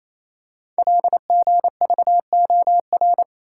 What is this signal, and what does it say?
Morse code, heard as a keyed tone, about 29 wpm, 700 Hz. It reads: LGVOR
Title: Code Groups - 5 Letters - 28wpm